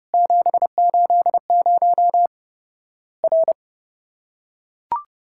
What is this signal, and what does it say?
Morse code, heard as a keyed tone, about 32 wpm, 700 Hz. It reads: 780 R
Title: Worked All Europe DX Contest - QTC Exchange - 30wpm